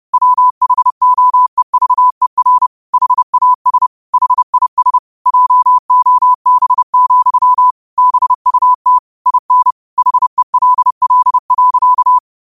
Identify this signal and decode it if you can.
Morse code, heard as a keyed tone, about 30 wpm, 1000 Hz. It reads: WHOEVER HAS HIS JOB, BUT IN HELL.
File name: qotdfiles / 30wpm